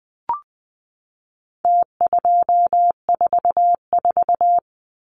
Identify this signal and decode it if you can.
Morse code, heard as a keyed tone, about 20 wpm, 700 Hz. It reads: T244
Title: Numbers - Contest Style - 4-Digit 20wpm